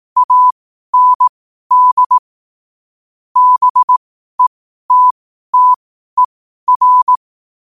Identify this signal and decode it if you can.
Morse code, heard as a keyed tone, about 18 wpm, spaced as for 12 wpm, 1000 Hz. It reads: AND BETTER
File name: qotdfiles / 12wpm